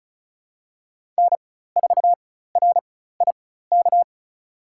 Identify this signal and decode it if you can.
Morse code, heard as a keyed tone, about 36 wpm, spaced as for 16 wpm, 700 Hz. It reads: N4RIK
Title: USA Call Signs - 5 Characters - 4x Character Spacing - 35wpm